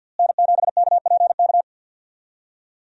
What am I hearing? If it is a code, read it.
Morse code, heard as a keyed tone, about 48 wpm, 700 Hz. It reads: N7KPK
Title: USA Call Signs - 5 Characters - Repeated 2x - 50wpm